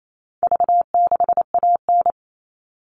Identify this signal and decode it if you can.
Morse code, heard as a keyed tone, about 28 wpm, 700 Hz. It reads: V6AN
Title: All Cut Numbers - 4-Digit 28wpm